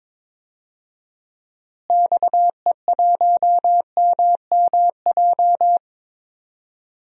Morse code, heard as a keyed tone, about 22 wpm, 700 Hz. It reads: XE1MMJ
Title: DXCC-100 Call Signs - 6 Characters - Repeated 2x - 22wpm